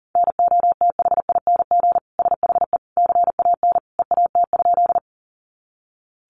Morse code, heard as a keyed tone, about 40 wpm, 700 Hz. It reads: NOTHING SHE CAN EAT?
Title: Sentences from Top 500 Words 40wpm